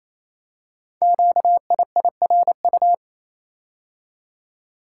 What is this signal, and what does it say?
Morse code, heard as a keyed tone, about 28 wpm, 700 Hz. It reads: QIIRU
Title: Code Groups - 5 Letters - 28wpm